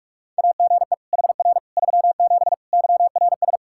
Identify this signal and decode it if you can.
Morse code, heard as a keyed tone, about 46 wpm, 700 Hz. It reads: AGE HR 37 YRS
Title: Practice QSO - Exchange 3 - More Info - 45wpm